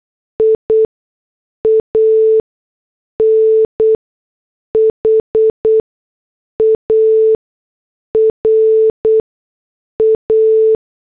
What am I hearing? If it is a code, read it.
Morse code, heard as a keyed tone, about 8 wpm, spaced as for 6 wpm, 440 Hz. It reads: IAN HARA